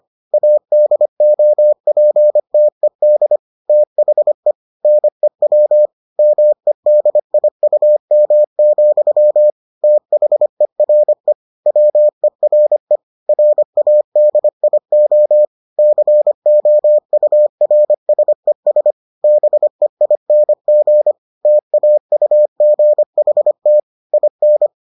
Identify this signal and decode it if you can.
Morse code, heard as a keyed tone, about 25 wpm, 600 Hz. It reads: ADOPTED THE NEW MEDIUM, THERE WERE RADIO COURSES BEING TAUGHT IN